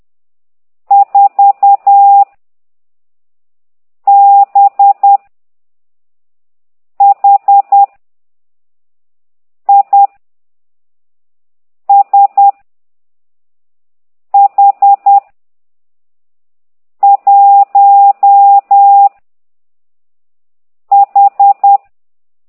Morse code, heard as a keyed tone, about 10 wpm, 800 Hz. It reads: 4 B H I S H 1 H